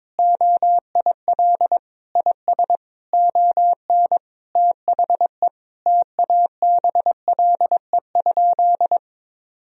Morse code, heard as a keyed tone, about 21 wpm, 700 Hz. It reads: OIL IS ON THE TABLE?
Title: Sentences from Top 1000 Words - Review Part 4 - 22wpm